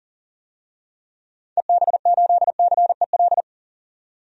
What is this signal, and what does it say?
Morse code, heard as a keyed tone, about 40 wpm, 700 Hz. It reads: EB8CEL
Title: DXCC-100 Call Signs - 6 Characters - 40wpm